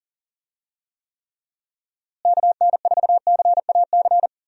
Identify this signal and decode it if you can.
Morse code, heard as a keyed tone, about 40 wpm, 700 Hz. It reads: KN4CAC
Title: USA Call Signs - 6 Characters - Repeated 2x - 40wpm